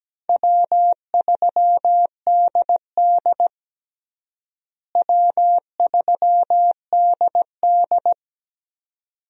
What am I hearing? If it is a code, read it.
Morse code, heard as a keyed tone, about 17 wpm, 700 Hz. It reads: W3DD W3DD
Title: USA Call Signs - 4 Characters - Repeated 2x - 17wpm